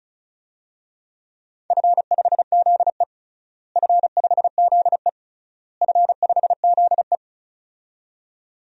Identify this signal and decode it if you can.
Morse code, heard as a keyed tone, about 34 wpm, 700 Hz. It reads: F5ZE F5ZE F5ZE
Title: DXCC-100 Call Signs - 4 Characters - Repeated 3x - 35wpm